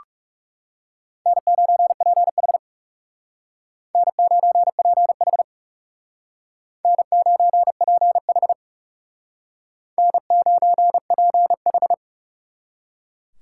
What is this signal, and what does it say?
Morse code, heard as a keyed tone, about 39 wpm, 700 Hz. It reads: N9PH N9PH N9PH N9PH